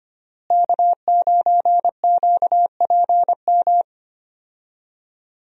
Morse code, heard as a keyed tone, about 25 wpm, 700 Hz. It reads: K9QPM
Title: USA Call Signs - 5 Characters - Repeated 2x - 25wpm